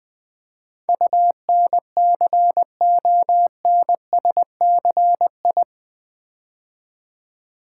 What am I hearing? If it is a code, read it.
Morse code, heard as a keyed tone, about 20 wpm, 700 Hz. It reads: UNCONSCI
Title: Build A Word - 4 Letters At A Time - 11-Letter Words - 20wpm